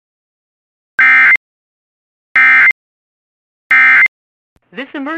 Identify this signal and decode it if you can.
Emergency Alert System (SAME) tones, heard as harsh buzzing bursts. It is heard from 1.0 to 4.1 s.